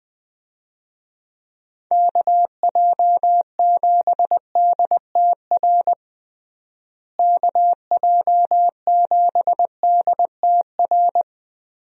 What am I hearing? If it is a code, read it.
Morse code, heard as a keyed tone, about 21 wpm, 700 Hz. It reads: KJ7DTR KJ7DTR
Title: USA Call Signs - 6 Characters - Repeated 3x - 20wpm